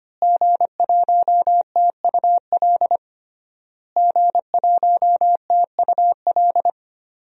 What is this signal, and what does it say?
Morse code, heard as a keyed tone, about 25 wpm, 700 Hz. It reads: G1TUL G1TUL
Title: DXCC-100 Call Signs - 5 Characters - Repeated 3x - 25wpm